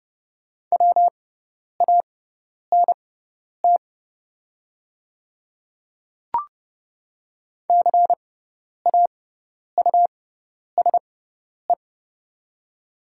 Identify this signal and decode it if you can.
Morse code, heard as a keyed tone, about 30 wpm, spaced as for 10 wpm, 690 Hz. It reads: WANT CAUSE
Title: 6x Character Spacing - Top 200 Words - 30wpm